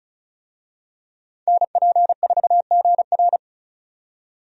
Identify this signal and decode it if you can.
Morse code, heard as a keyed tone, about 34 wpm, 700 Hz. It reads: NP4GR
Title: USA Call Signs - 5 Characters - 35wpm